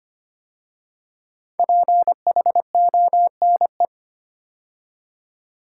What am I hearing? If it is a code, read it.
Morse code, heard as a keyed tone, about 25 wpm, 700 Hz. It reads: PHONE
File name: 5 Letter Words - First 3 Letters Given - 25wpm